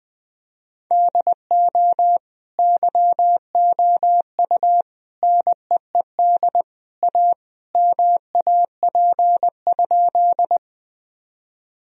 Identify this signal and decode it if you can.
Morse code, heard as a keyed tone, about 20 wpm, 700 Hz. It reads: DO YOU NEED A MAP?
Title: Sentences from Top 1000 Words - Review Part 2 - 20wpm